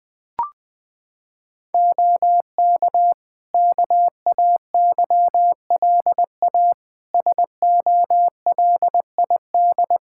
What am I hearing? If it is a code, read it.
Morse code, heard as a keyed tone, about 20 wpm, 700 Hz. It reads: OK KAYLA SOLID
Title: K1USN SST QSO Practice 20wpm